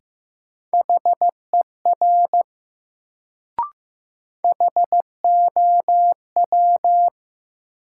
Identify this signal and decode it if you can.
Morse code, heard as a keyed tone, about 15 wpm, 700 Hz. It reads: HER HOW
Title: Sentences from Top 100 Words - New Word Drill - 15wpm